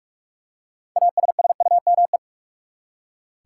Morse code, heard as a keyed tone, about 46 wpm, 700 Hz. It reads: ASSUME
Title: Six Letter Words - 45wpm